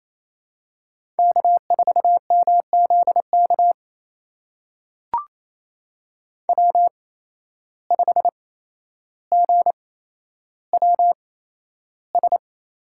Morse code, heard as a keyed tone, about 28 wpm, spaced as for 12 wpm, 700 Hz. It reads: K4MZK W5GWS
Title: USA Call Signs - 5 Characters - 8x Character Spacing - 28wpm